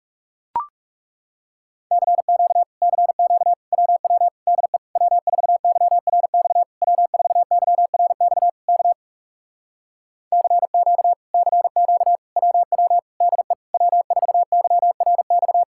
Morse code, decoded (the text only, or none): CQ CQ WW DE W4YRX W4YRX K CQ CQ WW DE W4YRX